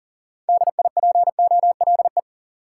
Morse code, heard as a keyed tone, about 40 wpm, 700 Hz. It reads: DIPOLE
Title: Top 180 QSO Elements 40wpm